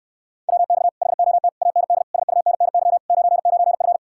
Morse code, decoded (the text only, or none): WX HOT AND SUNNY 99F